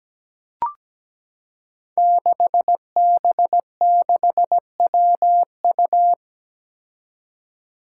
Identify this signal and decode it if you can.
Morse code, heard as a keyed tone, about 17 wpm, 700 Hz. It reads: 6B6WU
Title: Five Challenging Letter-Number Combinations 17wpm